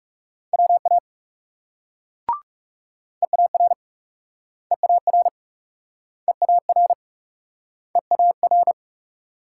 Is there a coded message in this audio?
Yes: Morse code from 0.5 to 8.7 s, heard as a keyed tone, about 42 wpm, 700 Hz. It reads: WA EAR EAR EAR EAR